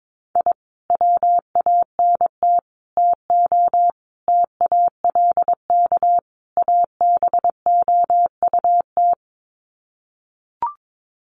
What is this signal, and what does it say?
Morse code, heard as a keyed tone, about 22 wpm, 700 Hz. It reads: I WANT TO TALK ABOUT
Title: Sets of 5 Words Encore 22wpm